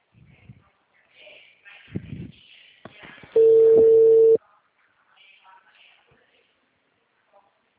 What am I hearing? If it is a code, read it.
Ringback tone (China, 450 Hz).